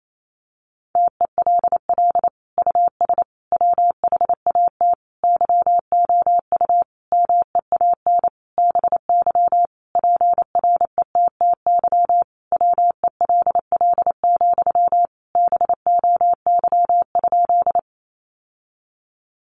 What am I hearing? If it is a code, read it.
Morse code, heard as a keyed tone, about 28 wpm, 700 Hz. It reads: TELL US WHAT YOU MEAN BY PRETTY WELL, BOY?
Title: Sentences from Top 1000 Words 28wpm